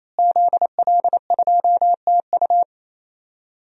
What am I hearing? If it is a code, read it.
Morse code, heard as a keyed tone, about 28 wpm, 700 Hz. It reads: ZL2TU
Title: DXCC-100 Call Signs - 5 Characters - Repeated 3x - 28wpm